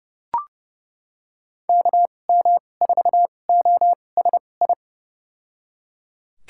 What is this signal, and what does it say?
Morse code, heard as a keyed tone, about 30 wpm, spaced as for 22 wpm, 700 Hz. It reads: KM4OSI
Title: USA Call Signs - 6 Characters - 2x Character Spacing - 30wpm